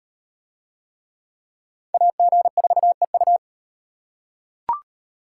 Morse code, heard as a keyed tone, about 36 wpm, 700 Hz. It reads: AG4EU